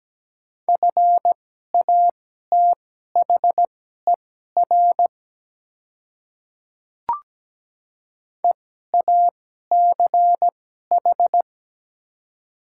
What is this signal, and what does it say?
Morse code, heard as a keyed tone, about 17 wpm, spaced as for 12 wpm, 700 Hz. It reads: FATHER EACH
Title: 2x Character Spacing - Top 200 Words - 17wpm